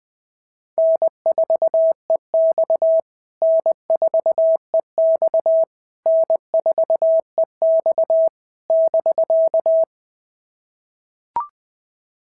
Morse code, heard as a keyed tone, about 18 wpm, 650 Hz. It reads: N4EX N4EX N4EX <BK>
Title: POTA QSO Practice - 20wpm